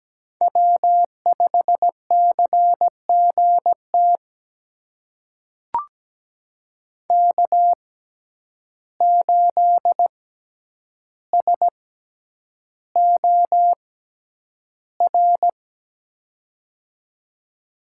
Morse code, heard as a keyed tone, about 17 wpm, spaced as for 9 wpm, 700 Hz. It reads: W5CGT K8SOR
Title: USA Call Signs - 5 Characters - 6x Character Spacing - 17wpm